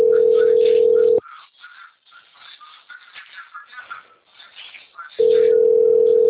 Ringback tone (North America), 440 + 480 Hz.